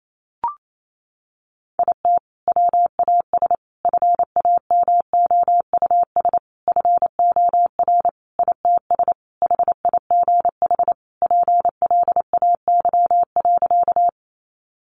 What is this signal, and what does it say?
Morse code, heard as a keyed tone, about 28 wpm, 700 Hz. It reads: IT WAS FAMOUS FOR ITS HIGH PLAY.
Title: Sentences from Top 1500 Words 28wpm